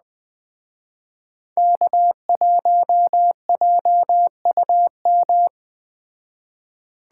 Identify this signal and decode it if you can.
Morse code, heard as a keyed tone, about 20 wpm, 700 Hz. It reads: K1JUM
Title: Speed Racing - USA 1x3 Call Signs 17wpm